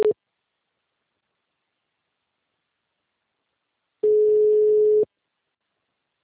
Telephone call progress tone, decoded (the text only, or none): ringback tone